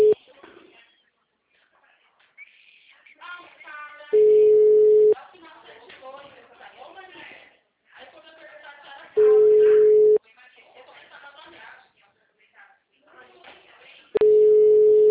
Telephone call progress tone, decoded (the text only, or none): ringback tone